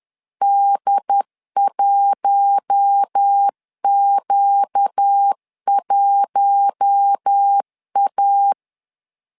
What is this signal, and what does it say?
Morse code, heard as a keyed tone, about 10 wpm, 800 Hz. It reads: D1Q1A
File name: LEZIONI / 4 Q1 10wpm 2448